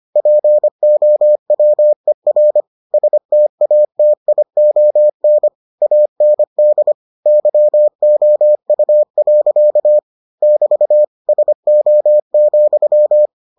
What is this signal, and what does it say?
Morse code, heard as a keyed tone, about 25 wpm, 600 Hz. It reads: POWER STATION AND YOU. <BT> SO,